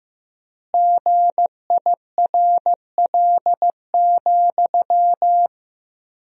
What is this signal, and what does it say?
Morse code, heard as a keyed tone, about 15 wpm, 700 Hz. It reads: GIRL,
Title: 4x Word-Spacing - Sentences from Top 400 Words - 15wpm